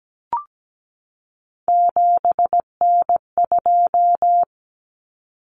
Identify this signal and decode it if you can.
Morse code, heard as a keyed tone, about 17 wpm, 700 Hz. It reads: 7N2
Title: Common Cut Numbers - 3-Digit 17wpm